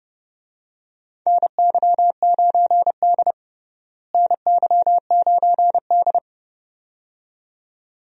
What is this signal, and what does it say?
Morse code, heard as a keyed tone, about 32 wpm, 700 Hz. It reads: NY9D NY9D
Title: USA Call Signs - 4 Characters - Repeated 2x - 30wpm